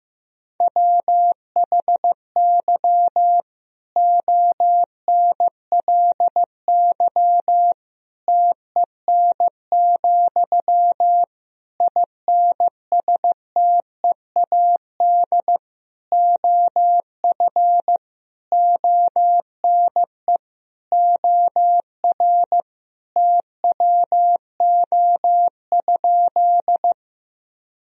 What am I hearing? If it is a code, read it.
Morse code, heard as a keyed tone, about 16 wpm, 700 Hz. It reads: WHY ONLY TEN, INSTEAD OF ONE OR TWO?
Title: Sentences from Top 1000 Words - Review Part 4 - 15wpm